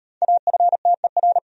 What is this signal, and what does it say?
Morse code, heard as a keyed tone, about 38 wpm, 700 Hz. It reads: AFTER